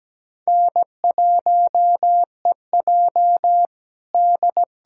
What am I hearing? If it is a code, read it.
Morse code, heard as a keyed tone, about 17 wpm, 700 Hz. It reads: N1EJ D
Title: USA Callsign DE Callsign - 4 Characters - 17wpm